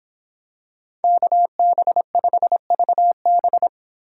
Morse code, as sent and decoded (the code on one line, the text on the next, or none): -.- -... ..... ...- -...
KB5VB